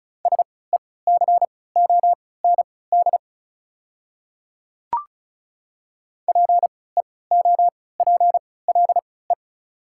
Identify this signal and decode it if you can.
Morse code, heard as a keyed tone, about 34 wpm, spaced as for 20 wpm, 700 Hz. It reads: SECOND PEOPLE
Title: 3x Character Spacing - Top 300 Words - 35wpm